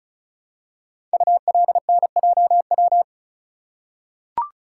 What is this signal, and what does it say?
Morse code, heard as a keyed tone, about 36 wpm, 700 Hz. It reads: ULNJW
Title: Code Groups - 5 Letters - 35wpm